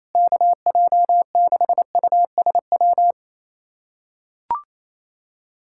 Morse code, heard as a keyed tone, about 28 wpm, 700 Hz. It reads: KJ6USW
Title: USA Callsign DE Callsign - 6 Characters - 28wpm